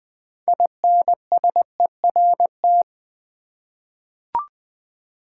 Morse code, heard as a keyed tone, about 20 wpm, 690 Hz. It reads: INSERT
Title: 6 Letter Words - First 3 Letters Given - 20wpm